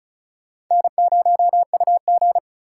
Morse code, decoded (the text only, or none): N0UG